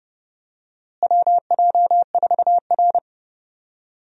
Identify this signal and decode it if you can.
Morse code, heard as a keyed tone, about 30 wpm, 700 Hz. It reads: WJ4R
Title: USA Call Signs - 4 Characters - Repeated 3x - 30wpm